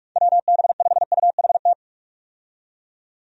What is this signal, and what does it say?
Morse code, heard as a keyed tone, about 46 wpm, 700 Hz. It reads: WB5UHT
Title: USA Call Signs - 6 Characters - 45wpm